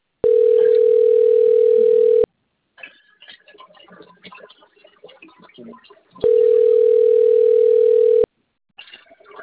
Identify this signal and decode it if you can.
Ringback tone (North America), 440 + 480 Hz.